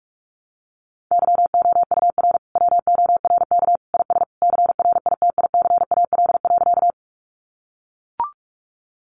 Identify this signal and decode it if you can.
Morse code, heard as a keyed tone, about 44 wpm, 700 Hz. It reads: YOUR WORK IS CRITICAL.
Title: Sentences from Top 1100 Words 45wpm